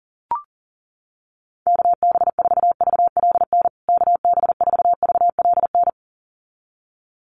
Morse code, heard as a keyed tone, about 38 wpm, 700 Hz. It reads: KB4VLN KB4VLN
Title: USA Call Signs Repeated 2x 40wpm